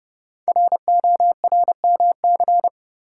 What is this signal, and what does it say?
Morse code, heard as a keyed tone, about 30 wpm, 700 Hz. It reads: RORMC
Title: Code Groups - 5 Letters - 30wpm